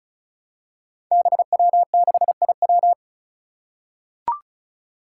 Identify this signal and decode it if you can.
Morse code, heard as a keyed tone, about 36 wpm, 700 Hz. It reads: BW6IW
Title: Five Challenging Letter-Number Combinations 35wpm